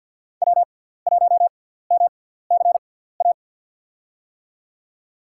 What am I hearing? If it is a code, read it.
Morse code, heard as a keyed tone, about 47 wpm, spaced as for 17 wpm, 700 Hz. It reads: W1MCA
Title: USA Call Signs - 6x Character Spacing - 50wpm